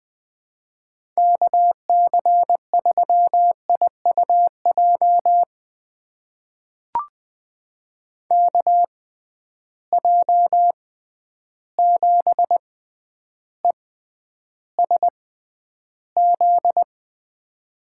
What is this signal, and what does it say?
Morse code, heard as a keyed tone, about 20 wpm, spaced as for 10 wpm, 700 Hz. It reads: KC3IUJ KJ7ESZ
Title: USA Call Signs - 6 Characters - 6x Character Spacing - 20wpm